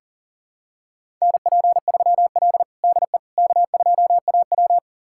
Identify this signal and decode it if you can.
Morse code, heard as a keyed tone, about 40 wpm, 700 Hz. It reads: NP3L DE K2AW
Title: USA Callsign DE Callsign - 4 Characters - 40wpm